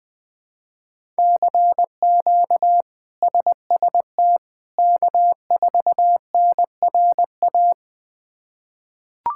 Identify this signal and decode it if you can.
Morse code, heard as a keyed tone, about 20 wpm, 700 Hz. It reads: CQ SST K4NRA
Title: K1USN SST QSO Practice - 20wpm